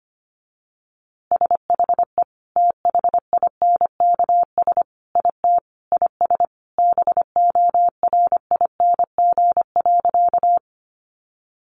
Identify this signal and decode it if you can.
Morse code, heard as a keyed tone, about 25 wpm, 700 Hz. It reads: SHE THINKS IT IS BORING.
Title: Easy Phrases 25wpm